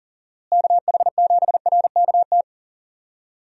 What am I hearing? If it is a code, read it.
Morse code, heard as a keyed tone, about 40 wpm, 700 Hz. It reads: KH7RKT
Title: USA Call Signs - 6 Characters - Repeated 3x - 40wpm